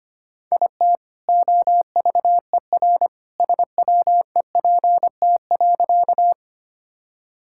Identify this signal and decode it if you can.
Morse code, heard as a keyed tone, about 25 wpm, 700 Hz. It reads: IT OVER SWEPT.
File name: practice / 25wpm / All-Age Appropriate Jokes - 25wpm